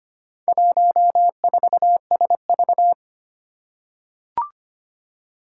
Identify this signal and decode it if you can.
Morse code, heard as a keyed tone, about 25 wpm, 700 Hz. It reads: 14SV
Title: Four Challenging Letter-Number Combinations 25wpm